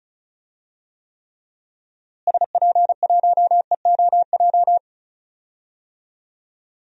Morse code, heard as a keyed tone, about 36 wpm, 700 Hz. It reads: SP1EOJ
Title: DXCC-100 Call Signs - 6 Characters - Repeated 3x - 35wpm